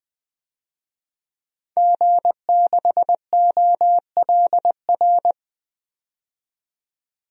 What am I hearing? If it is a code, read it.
Morse code, heard as a keyed tone, about 20 wpm, 700 Hz. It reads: G6OLR
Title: DXCC-100 Call Signs - 5 Characters - 20wpm